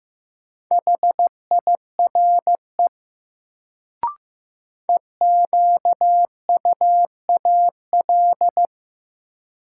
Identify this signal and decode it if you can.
Morse code, heard as a keyed tone, about 15 wpm, 700 Hz. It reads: HIRE EQUAL
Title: Sentences from Top 1400 Words - New Word Drill - 15wpm